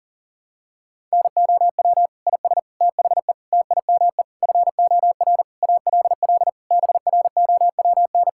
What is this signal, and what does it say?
Morse code, heard as a keyed tone, about 40 wpm, 700 Hz. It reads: NOW IS THE TIME FOR ALL BROWN
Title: Pangram Challenge v2 - 40wpm